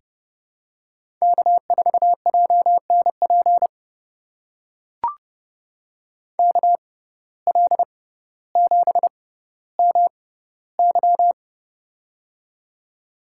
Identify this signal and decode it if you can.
Morse code, heard as a keyed tone, about 30 wpm, spaced as for 15 wpm, 700 Hz. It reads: K4JNP KL7MY
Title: USA Call Signs - 5 Characters - 6x Character Spacing - 30wpm